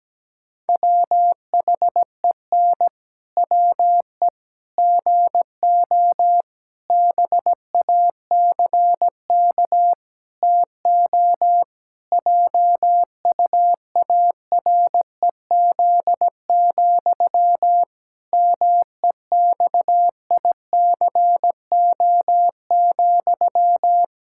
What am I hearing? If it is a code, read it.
Morse code, heard as a keyed tone, about 17 wpm, 700 Hz. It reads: WHEN WE GO BACK TO JUAREZ, MEXICO,
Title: Pangram Challenge v2 - 17wpm